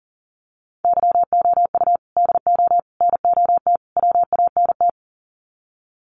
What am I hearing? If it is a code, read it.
Morse code, heard as a keyed tone, about 40 wpm, 700 Hz. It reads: YOU DO NOT WANT
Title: Sets of 4 Words 40wpm